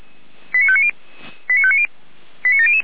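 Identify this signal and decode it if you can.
Five-tone selective call (ZVEI), heard as quick runs of five tones, tones of about 70 ms. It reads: 89590 89590 89700
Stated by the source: Five-tone ZVEI selective call received on an SDR.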